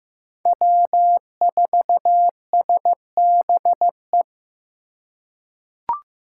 Morse code, heard as a keyed tone, about 16 wpm, 700 Hz. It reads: W4SBE
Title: USA Call Signs - 5 Characters - 4x Character Spacing - 15wpm